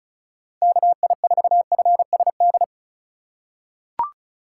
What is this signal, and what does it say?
Morse code, heard as a keyed tone, about 36 wpm, 700 Hz. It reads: KI4FSD
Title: USA Call Signs - 6 Characters - 2x Character Spacing - 35wpm